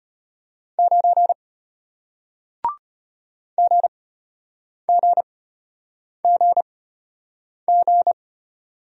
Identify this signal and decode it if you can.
Morse code, heard as a keyed tone, about 33 wpm, spaced as for 8 wpm, 700 Hz. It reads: 9 GGGG